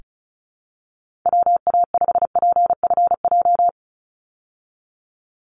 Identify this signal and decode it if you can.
Morse code, heard as a keyed tone, about 34 wpm, 700 Hz. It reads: WA5PFJ
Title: Call Signs Repeated 3x - 35wpm